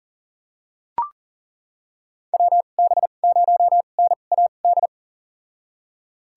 Morse code, heard as a keyed tone, about 40 wpm, spaced as for 29 wpm, 700 Hz. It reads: WB0NAD
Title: USA Call Signs - 6 Characters - 2x Character Spacing - 40wpm